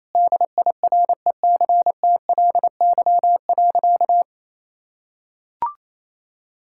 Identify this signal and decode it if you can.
Morse code, heard as a keyed tone, about 28 wpm, 700 Hz. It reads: DIRECTLY.
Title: Sentences from Top 1000 Words - Review Part 5 - 28wpm